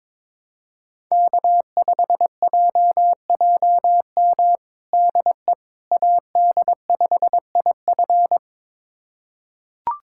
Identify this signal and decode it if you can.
Morse code, heard as a keyed tone, about 22 wpm, 700 Hz. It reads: K5JJM DE AD5IF
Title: USA Callsign DE Callsign - 5 Characters - 22wpm